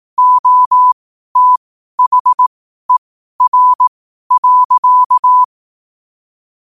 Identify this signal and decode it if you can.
Morse code, heard as a keyed tone, about 18 wpm, spaced as for 12 wpm, 1000 Hz. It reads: OTHER.
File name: qotdfiles / 12wpm